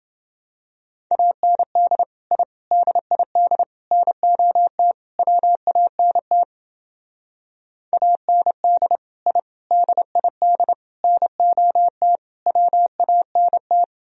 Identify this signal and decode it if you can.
Morse code, heard as a keyed tone, about 28 wpm, 700 Hz. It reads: AND I DID NOT WANT AND I DID NOT WANT